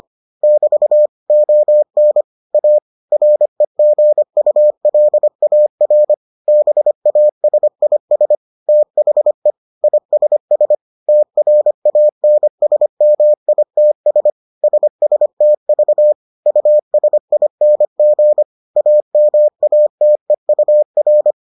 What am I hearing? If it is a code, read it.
Morse code, heard as a keyed tone, about 25 wpm, 600 Hz. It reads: <BT> ON A REGULAR BASIS THE ISS TRANSMITS SSTV USING AMATEUR